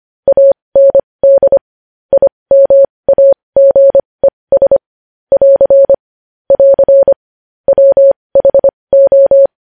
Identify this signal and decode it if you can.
Morse code, heard as a keyed tone, about 25 wpm, spaced as for 20 wpm, 550 Hz. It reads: AND IMAGES <AR> <AR> WHO
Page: News Headlines in Morse Code at 20 WPM